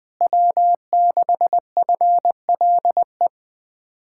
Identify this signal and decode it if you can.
Morse code, heard as a keyed tone, about 20 wpm, 700 Hz. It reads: W6FLE
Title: Speed-Racing Common QSO Phrases - 17wpm